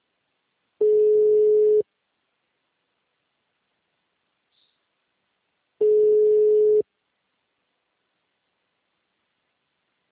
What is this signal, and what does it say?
Ringback tone (Europe, 425 Hz).